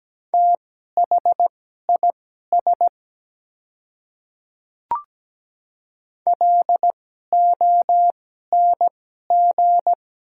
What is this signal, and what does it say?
Morse code, heard as a keyed tone, about 17 wpm, spaced as for 12 wpm, 700 Hz. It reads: THIS LONG
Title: 2x Character Spacing - Top 100 Words - 17wpm